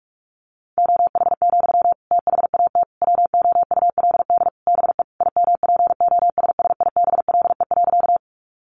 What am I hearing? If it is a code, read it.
Morse code, heard as a keyed tone, about 44 wpm, 700 Hz. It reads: OH, THAT WOULD BE IMPOSSIBLE.
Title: Sentences from Top 1100 Words 45wpm